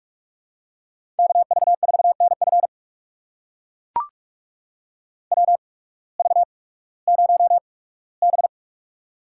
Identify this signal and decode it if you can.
Morse code, heard as a keyed tone, about 41 wpm, spaced as for 20 wpm, 700 Hz. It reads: KV4NF WV0B
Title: USA Call Signs - 8x Character Spacing - 45wpm